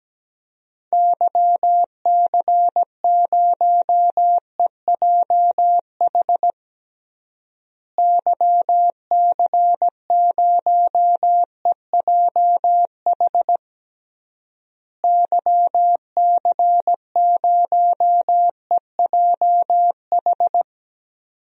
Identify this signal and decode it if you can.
Morse code, heard as a keyed tone, about 17 wpm, 700 Hz. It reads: YC0EJH YC0EJH YC0EJH
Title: DXCC-100 Call Signs - 6 Characters - Repeated 3x - 17wpm